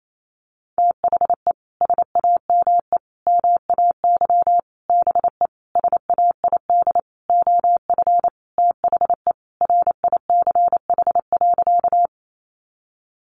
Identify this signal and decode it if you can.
Morse code, heard as a keyed tone, about 28 wpm, 700 Hz. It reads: THE SAME MAY BE SAID OF THE RICH.
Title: Sentences from Top 1100 Words 28wpm